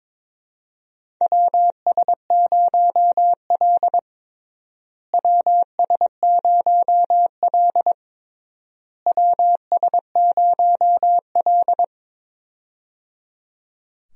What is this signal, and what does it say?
Morse code, heard as a keyed tone, about 22 wpm, 700 Hz. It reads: WS0L WS0L WS0L
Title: USA Call Signs - 4 Characters - Repeated 3x - 22wpm